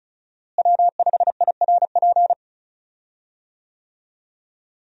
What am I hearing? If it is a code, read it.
Morse code, heard as a keyed tone, about 35 wpm, 700 Hz. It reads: W5IRP
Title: USA Call Signs - 5 Characters - Repeated 3x - 35wpm